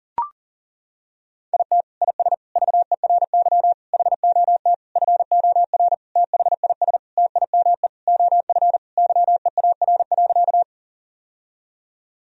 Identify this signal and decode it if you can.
Morse code, heard as a keyed tone, about 40 wpm, 700 Hz. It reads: IT IS VERY HOT FOR THIS TIME OF YEAR.
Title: Sentences from Top 1000 Words - Review Part 3 - 40wpm